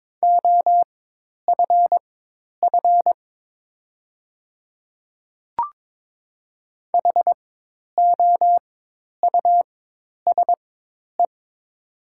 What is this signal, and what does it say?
Morse code, heard as a keyed tone, about 22 wpm, spaced as for 10 wpm, 700 Hz. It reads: OFF HOUSE
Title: 4x Character Spacing - Top 200 Words - 22wpm